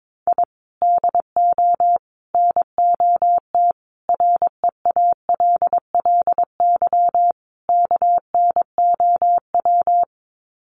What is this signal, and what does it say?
Morse code, heard as a keyed tone, about 22 wpm, 700 Hz. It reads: I DO NOT REALLY KNOW
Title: Sets of 5 Words Encore 22wpm